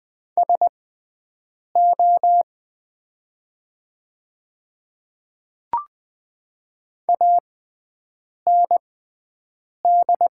Morse code, heard as a keyed tone, about 20 wpm, spaced as for 7 wpm, 700 Hz. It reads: SO AND
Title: 6x Character Spacing - Top 100 Words - 20wpm